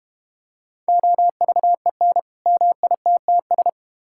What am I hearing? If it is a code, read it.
Morse code, heard as a keyed tone, about 32 wpm, 700 Hz. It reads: OVEN MITTS